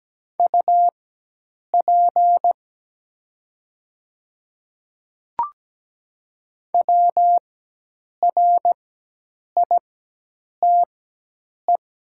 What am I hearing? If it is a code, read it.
Morse code, heard as a keyed tone, about 17 wpm, spaced as for 8 wpm, 700 Hz. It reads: UP WRITE
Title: 4x Character Spacing - Top 100 Words - 17wpm